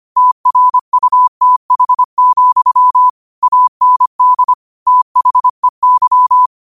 Morse code, heard as a keyed tone, about 25 wpm, 1000 Hz. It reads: TRUTH, AND THEY
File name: qotdfiles / 25wpm